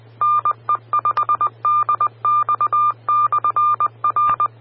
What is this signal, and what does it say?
Morse code, heard as a keyed tone, about 20 wpm, 1200 Hz. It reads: NE5DX/R
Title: Austin, TX Amateur Repeaters